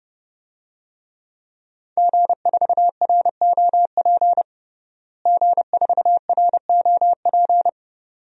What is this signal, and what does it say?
Morse code, heard as a keyed tone, about 30 wpm, 700 Hz. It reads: G4ROP G4ROP
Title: DXCC-100 Call Signs - 5 Characters - Repeated 2x - 30wpm